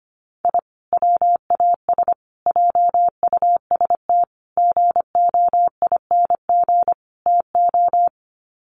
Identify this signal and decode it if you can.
Morse code, heard as a keyed tone, about 25 wpm, 700 Hz. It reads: I WAS JUST GOING TO
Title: Sets of 5 Words Encore 25wpm